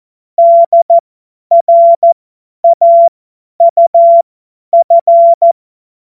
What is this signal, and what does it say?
Morse code, heard as a keyed tone, about 14 wpm, spaced as for 10 wpm, 680 Hz. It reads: DRAUF
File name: Morsekurs / Witze / Witz 10. Mai 2022 50 F70